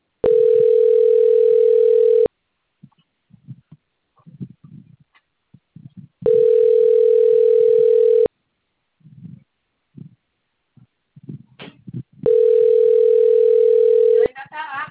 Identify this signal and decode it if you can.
Ringback tone (North America), 440 + 480 Hz, with speech.
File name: rg-1600-unknown- - - .30050